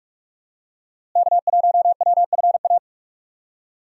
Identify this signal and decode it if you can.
Morse code, heard as a keyed tone, about 45 wpm, 700 Hz. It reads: K1WFA
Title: USA Call Signs - 5 Characters - Repeated 2x - 45wpm